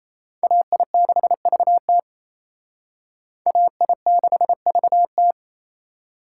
Morse code, heard as a keyed tone, about 35 wpm, 700 Hz. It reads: AI6VT AI6VT